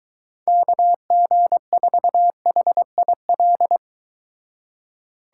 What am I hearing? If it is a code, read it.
Morse code, heard as a keyed tone, about 23 wpm, 700 Hz. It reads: KG4HIL